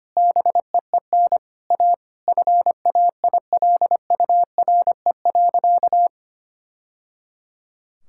Morse code, heard as a keyed tone, about 25 wpm, 700 Hz. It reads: BEEN A FAILURE.
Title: Sentences from Top 1000 Words - Review Part 6 - 25wpm